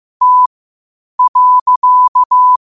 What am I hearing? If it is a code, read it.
Morse code, heard as a keyed tone, about 15 wpm, spaced as for 8 wpm, 1000 Hz. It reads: T.
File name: qotdfiles / 8wpm